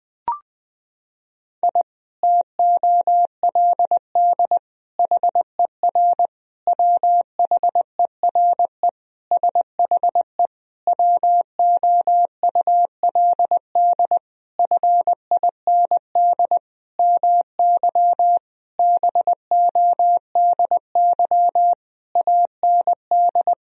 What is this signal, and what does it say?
Morse code, heard as a keyed tone, about 20 wpm, 700 Hz. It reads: I TOLD HER WHERE SHE WOULD FIND MY BODY AND
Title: Sentences from Top 1000 Words - Review Part 2 - 20wpm